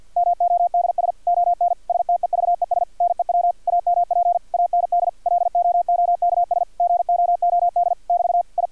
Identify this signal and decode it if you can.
Morse code, heard as a keyed tone, about 50 wpm, 700 Hz, with radio noise. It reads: MODS ON STEVES NEW AMP AND LOOKS GOOD <BT> I
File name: archived / CRSnet